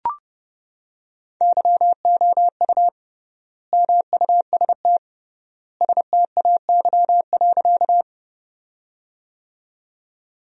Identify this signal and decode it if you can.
Morse code, heard as a keyed tone, about 30 wpm, 700 Hz. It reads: YOU MUST STAY.
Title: 3x Word-Spacing - Sentences from Top 500 Words - 30wpm